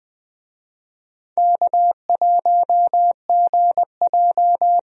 Morse code, heard as a keyed tone, about 20 wpm, 700 Hz. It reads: K1GJ
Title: USA Call Signs - 4 Characters - Repeated 2x - 20wpm